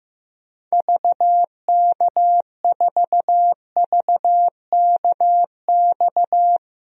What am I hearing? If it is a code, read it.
Morse code, heard as a keyed tone, about 15 wpm, 700 Hz. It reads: VK4VKX
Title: DXCC-100 Call Signs - 6 Characters - Repeated 2x - 15wpm